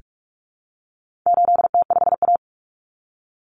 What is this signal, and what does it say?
Morse code, heard as a keyed tone, about 44 wpm, 700 Hz. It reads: 8T5A
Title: Common Cut Numbers - 4-Digit 45wpm